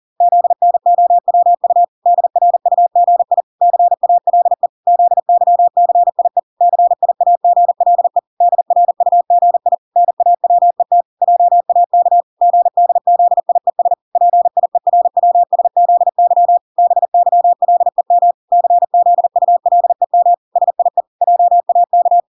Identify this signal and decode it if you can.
Morse code, heard as a keyed tone, about 40 wpm, 700 Hz. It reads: ZNOWU DRUGI CALE ZYCIE CIAGLE DRUGI NAWET JAK GDZIES PIERWSZY BYLEM CZULEM SIE JAK
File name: 40wpm